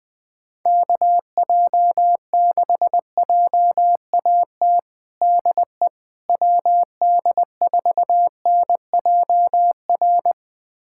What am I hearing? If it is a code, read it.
Morse code, heard as a keyed tone, about 20 wpm, 700 Hz. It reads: KJ6JAT DE WD4NJR
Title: USA Callsign DE Callsign - 6 Characters - 20wpm